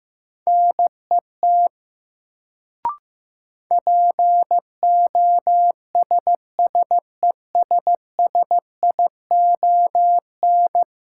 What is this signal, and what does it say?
Morse code, heard as a keyed tone, about 15 wpm, 700 Hz. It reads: NET POSSESSION
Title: Sentences from Top 1400 Words - New Word Drill Alt - 15wpm